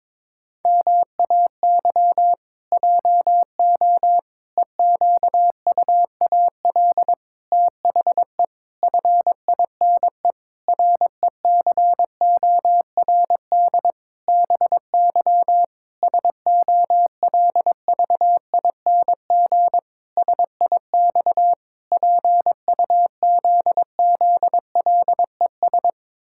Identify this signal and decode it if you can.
Morse code, heard as a keyed tone, about 22 wpm, 700 Hz. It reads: MAY JO EQUAL THE FINE RECORD BY SOLVING SIX PUZZLES
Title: Pangram Challenge v2 - 22wpm